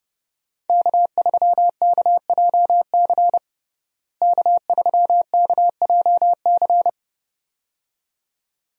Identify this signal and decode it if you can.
Morse code, heard as a keyed tone, about 30 wpm, 700 Hz. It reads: K3KJC K3KJC
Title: USA Call Signs - 5 Characters - Repeated 2x - 30wpm